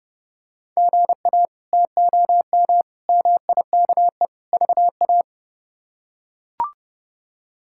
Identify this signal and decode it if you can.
Morse code, heard as a keyed tone, about 30 wpm, 700 Hz. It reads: GA TOM MIKE VA
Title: K1USN SST QSO Practice - 30wpm